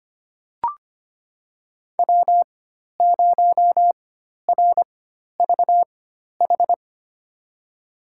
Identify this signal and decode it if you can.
Morse code, heard as a keyed tone, about 25 wpm, spaced as for 12 wpm, 700 Hz. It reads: W0RVH
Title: USA Call Signs - 5 Characters - 4x Character Spacing - 25wpm